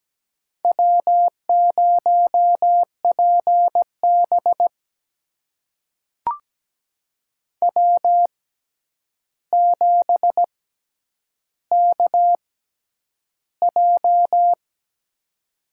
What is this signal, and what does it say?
Morse code, heard as a keyed tone, about 17 wpm, spaced as for 9 wpm, 700 Hz. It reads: W0PB W7KJ
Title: USA Call Signs - 6x Character Spacing - 17wpm